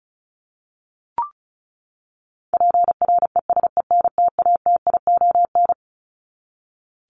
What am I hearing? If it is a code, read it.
Morse code, heard as a keyed tone, about 34 wpm, 700 Hz. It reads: PRESENTATION
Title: Top 1501-1750 Words 35wpm